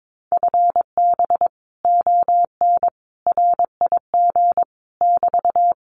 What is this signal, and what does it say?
Morse code, heard as a keyed tone, about 22 wpm, 700 Hz. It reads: FB ON RIG <BT>
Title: Common QSO Phrases Encore 22wpm